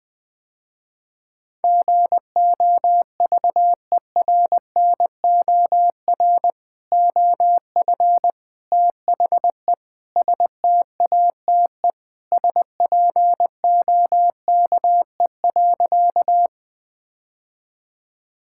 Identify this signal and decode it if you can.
Morse code, heard as a keyed tone, about 21 wpm, 700 Hz. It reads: GOVERNOR OF THE STATE SPOKE.
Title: Sentences with 3-Word Phrases - Spoken Beginning - 20wpm